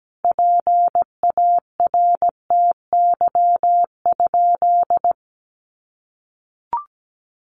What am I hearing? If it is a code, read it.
Morse code, heard as a keyed tone, about 17 wpm, 700 Hz. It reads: PARTY?
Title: Sentences from Top 1400 Words 17wpm